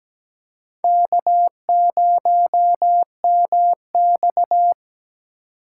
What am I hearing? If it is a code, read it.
Morse code, heard as a keyed tone, about 17 wpm, 700 Hz. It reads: K0MX
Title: USA Call Signs - 2x Character Spacing - 17wpm